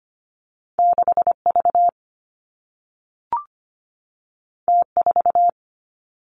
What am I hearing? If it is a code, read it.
Morse code, heard as a keyed tone, about 25 wpm, 700 Hz. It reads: 6V T4
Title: Two Challenging Letter-Number Combinations 25wpm